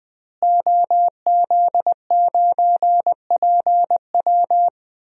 Morse code, heard as a keyed tone, about 20 wpm, 700 Hz. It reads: OZ9PW
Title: DXCC-100 Call Signs - 5 Characters - Repeated 2x - 20wpm